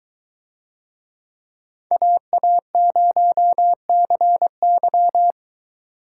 Morse code, heard as a keyed tone, about 23 wpm, 700 Hz. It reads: AA0CY